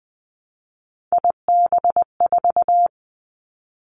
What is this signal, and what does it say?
Morse code, heard as a keyed tone, about 20 wpm, 700 Hz. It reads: IB4
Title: Three Challenging Letter-Number Combinations 20wpm